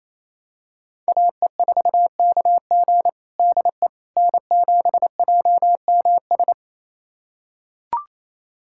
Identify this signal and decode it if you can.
Morse code, heard as a keyed tone, about 28 wpm, 690 Hz. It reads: AE4KG DE N7JMS
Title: USA Callsign DE Callsign - 5 Characters - 28wpm